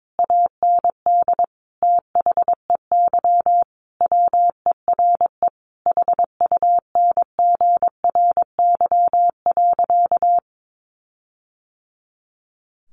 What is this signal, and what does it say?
Morse code, heard as a keyed tone, about 22 wpm, 700 Hz. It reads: AND THEY WERE HUNGRY.
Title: Sentences from Top 1400 Words 22wpm